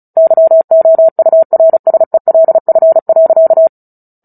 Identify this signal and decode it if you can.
Morse code, heard as a keyed tone, about 34 wpm, 650 Hz. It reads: YOURSELF.
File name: Book Two / 09-Chapter 9 - 35 WPM